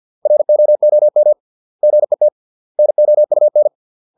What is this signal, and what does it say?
Morse code, heard as a keyed tone, about 50 wpm, 600 Hz. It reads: ROOM GET NOUN